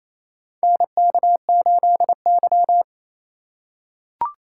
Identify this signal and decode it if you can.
Morse code, heard as a keyed tone, about 28 wpm, 700 Hz. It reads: NK8Y
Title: USA Call Signs - 4 Characters - 4x Character Spacing - 28wpm